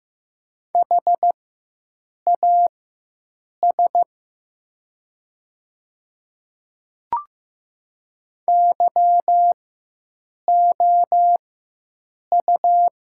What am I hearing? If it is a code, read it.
Morse code, heard as a keyed tone, about 15 wpm, spaced as for 7 wpm, 700 Hz. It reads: HAS YOU
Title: 4x Character Spacing - Top 100 Words - 15wpm